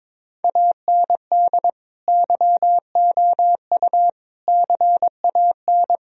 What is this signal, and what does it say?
Morse code, heard as a keyed tone, about 21 wpm, 700 Hz. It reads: AND YOU CAN
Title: Sentences from Top 1000 Words - Review Part 2 - 22wpm